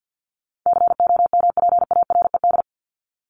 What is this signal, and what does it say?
Morse code, heard as a keyed tone, about 53 wpm, 700 Hz. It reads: COMPARED
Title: ED Suffix - Top 100 Words with Base Words 50wpm